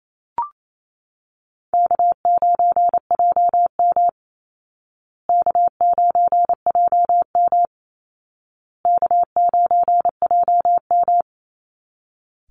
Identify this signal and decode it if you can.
Morse code, heard as a keyed tone, about 28 wpm, 700 Hz. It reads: K9JM K9JM K9JM
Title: Easy Call Signs Repeated 3x - 28wpm